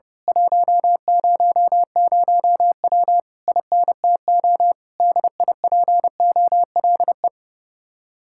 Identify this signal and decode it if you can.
Morse code, heard as a keyed tone, about 28 wpm, 700 Hz. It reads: 100W INTO DIPOLE
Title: Speed-Racing Common QSO Phrases - 30wpm